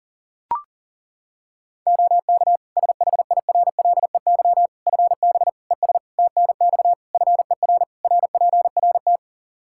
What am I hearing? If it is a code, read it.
Morse code, heard as a keyed tone, about 40 wpm, 700 Hz. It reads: OK SHIRLEY FB ES TNX FER RPRT
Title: Practice QSO - Exchange 2 - Info - 40wpm